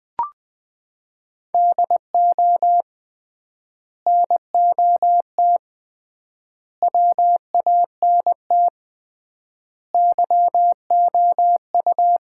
Morse code, heard as a keyed tone, about 20 wpm, 700 Hz. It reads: DO NOT WANT YOU
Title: 3x Word-Spacing - Sets of 5 Words - 20wpm